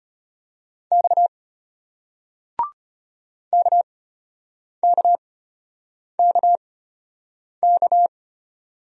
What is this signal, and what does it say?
Morse code, heard as a keyed tone, about 34 wpm, spaced as for 8 wpm, 690 Hz. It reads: X KKKK